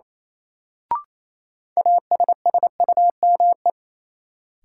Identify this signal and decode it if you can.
Morse code, heard as a keyed tone, about 28 wpm, 700 Hz. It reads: ASSUME
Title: Sentences from Top 1500 Words - New Word Drill Alt - 28wpm